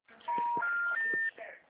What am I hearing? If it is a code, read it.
Special information tone, with noise.
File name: out- -2009